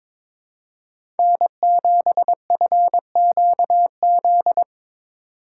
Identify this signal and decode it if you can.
Morse code, heard as a keyed tone, about 22 wpm, 700 Hz. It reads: N7FQZ
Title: USA Call Signs - 5 Characters - 3x Character Spacing - 22wpm